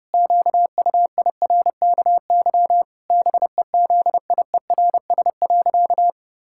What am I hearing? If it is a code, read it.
Morse code, heard as a keyed tone, about 30 wpm, 700 Hz. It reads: QUIRKY BEZIERS.
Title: Pangram Challenge v2 - 30wpm